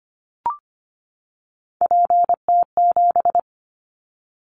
Morse code, heard as a keyed tone, about 25 wpm, 700 Hz. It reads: PT7
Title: Callsign Prefixes for Top 50 DXCC - 25wpm